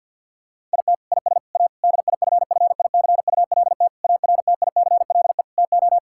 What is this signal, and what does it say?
Morse code, heard as a keyed tone, about 50 wpm, 700 Hz. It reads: IT IS A DIFFICULT ARTICLE TO
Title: Sentences from Top 1000 Words - Review Part 5 - 50wpm